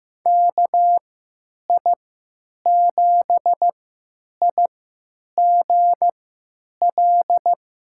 Morse code, heard as a keyed tone, about 15 wpm, spaced as for 9 wpm, 700 Hz. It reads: KI7IGL
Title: USA Call Signs - 6 Characters - 3x Character Spacing - 15wpm